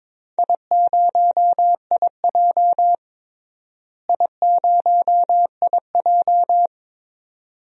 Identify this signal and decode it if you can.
Morse code, heard as a keyed tone, about 22 wpm, 700 Hz. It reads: I0IJ I0IJ
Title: DXCC-100 Call Signs - 4 Characters - Repeated 2x - 22wpm